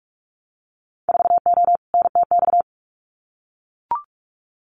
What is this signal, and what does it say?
Morse code, heard as a keyed tone, about 44 wpm, 700 Hz. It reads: 4O NTX
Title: Winter Field Day Exchanges 45wpm